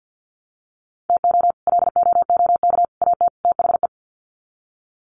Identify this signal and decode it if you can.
Morse code, heard as a keyed tone, about 50 wpm, 700 Hz. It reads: TO LOOK AT THE
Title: Sets of 4 Words 50wpm